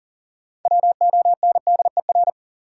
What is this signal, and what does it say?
Morse code, heard as a keyed tone, about 40 wpm, 700 Hz. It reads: WONDER
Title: Sentences from Top 500 Words - New Word Drill - 40wpm